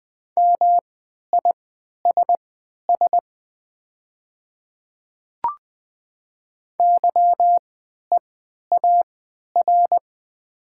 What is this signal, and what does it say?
Morse code, heard as a keyed tone, about 20 wpm, spaced as for 11 wpm, 700 Hz. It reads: MISS YEAR
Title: 3x Character Spacing - Top 500 Words - 20wpm